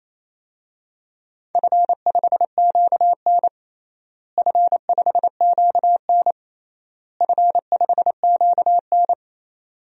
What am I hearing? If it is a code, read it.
Morse code, heard as a keyed tone, about 28 wpm, 700 Hz. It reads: F5QN F5QN F5QN
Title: DXCC-100 Call Signs - 4 Characters - Repeated 3x - 28wpm